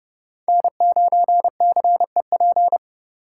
Morse code, heard as a keyed tone, about 30 wpm, 700 Hz. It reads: N9CEP